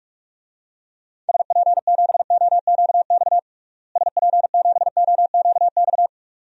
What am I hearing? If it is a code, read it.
Morse code, heard as a keyed tone, about 46 wpm, 700 Hz. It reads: SP7OQX SP7OQX
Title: DXCC-100 Call Signs - 6 Characters - Repeated 2x - 45wpm